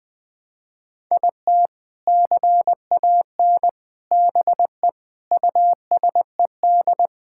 Morse code, heard as a keyed tone, about 20 wpm, 700 Hz. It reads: IT CAN BE USED
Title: Sentences with 4-Word Phrases - 20wpm